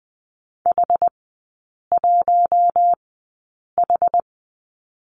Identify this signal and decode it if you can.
Morse code, heard as a keyed tone, about 20 wpm, spaced as for 8 wpm, 690 Hz. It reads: H1H
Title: 093 Lesson 16e - Letter c - Three Characters 20wpm